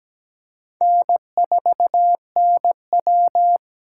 Morse code, heard as a keyed tone, about 17 wpm, 700 Hz. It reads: N4NW
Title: K1USN SST QSO Practice - 17wpm